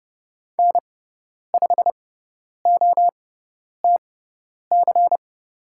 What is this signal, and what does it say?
Morse code, heard as a keyed tone, about 30 wpm, spaced as for 10 wpm, 690 Hz. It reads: N5OTC
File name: Lesson 18g - Letter m - Call Signs 30wpm